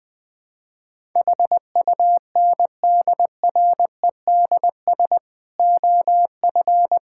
Morse code, heard as a keyed tone, about 21 wpm, 700 Hz. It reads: HUNDREDS OF
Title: Build A Phrase - 5-Word Phrases - 20wpm